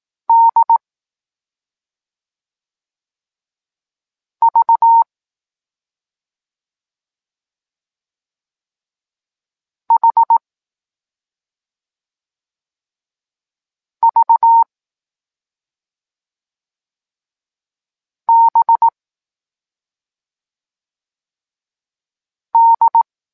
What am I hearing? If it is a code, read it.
Morse code, heard as a keyed tone, about 18 wpm, 920 Hz. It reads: D V H V B D
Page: Tekst 14 – doskonalenie odbioru poznanych liter